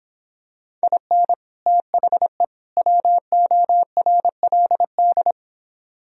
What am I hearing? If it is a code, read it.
Morse code, heard as a keyed tone, about 26 wpm, 700 Hz. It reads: IN THE WORLD